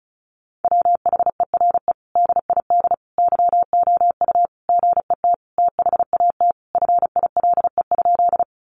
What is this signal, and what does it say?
Morse code, heard as a keyed tone, about 34 wpm, 700 Hz. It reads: WHERE DID YOU GET THAT FILE?
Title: Sentences from Top 1200 Words 35wpm